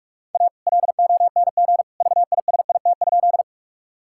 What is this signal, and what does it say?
Morse code, heard as a keyed tone, about 41 wpm, 700 Hz. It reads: A LONG VISIT?
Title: Sentences from Top 1000 Words - Review Part 6 - 45wpm